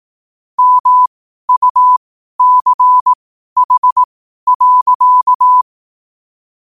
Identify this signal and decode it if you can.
Morse code, heard as a keyed tone, about 18 wpm, spaced as for 12 wpm, 1000 Hz. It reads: MUCH.
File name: qotdfiles / 12wpm